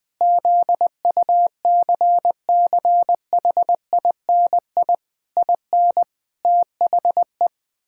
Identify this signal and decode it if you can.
Morse code, heard as a keyed tone, about 20 wpm, 700 Hz. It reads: ZUCCHINI IN THE
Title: Pangram Challenge v2 - 20wpm